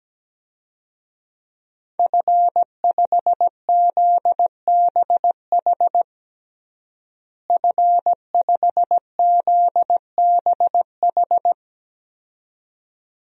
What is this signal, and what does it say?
Morse code, heard as a keyed tone, about 17 wpm, 700 Hz. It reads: F5ZBH F5ZBH
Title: DXCC-100 Call Signs - 5 Characters - Repeated 2x - 17wpm